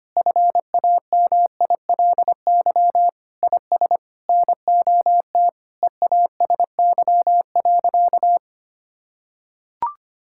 Morse code, heard as a keyed tone, about 25 wpm, 700 Hz. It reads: FAMILY IS NOT EASY.
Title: Sentences from Top 1000 Words - Review Part 6 - 25wpm